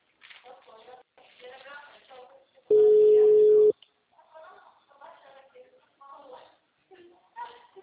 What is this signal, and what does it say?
Ringback tone (Europe, 425 Hz), with speech.
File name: out- -2039- - - .37110